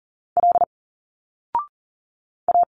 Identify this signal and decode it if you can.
Morse code, heard as a keyed tone, about 40 wpm, 700 Hz. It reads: L A
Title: Single Letters 40wpm